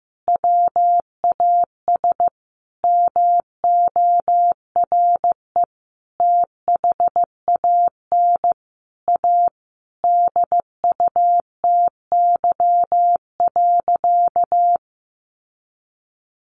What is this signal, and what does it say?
Morse code, heard as a keyed tone, about 15 wpm, 700 Hz. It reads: WAS MORE THAN A DUTY.
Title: Sentences from Top 1500 Words 15wpm